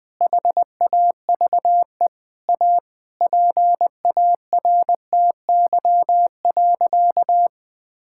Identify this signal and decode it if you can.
Morse code, heard as a keyed tone, about 20 wpm, 700 Hz. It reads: HAVE A PARTY.
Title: Sentences with 4-Word Phrases - 20wpm